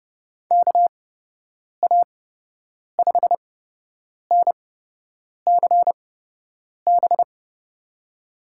Morse code, heard as a keyed tone, about 30 wpm, spaced as for 8 wpm, 700 Hz. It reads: KA5NCB
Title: USA Call Signs - 6 Characters - 8x Character Spacing - 30wpm